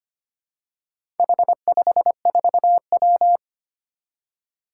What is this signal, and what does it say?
Morse code, heard as a keyed tone, about 25 wpm, 700 Hz. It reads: H54W
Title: Four Challenging Letter-Number Combinations 25wpm